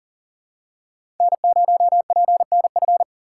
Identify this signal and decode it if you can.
Morse code, heard as a keyed tone, about 39 wpm, 700 Hz. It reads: N0PNF
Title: USA Call Signs - 5 Characters - 2x Character Spacing - 40wpm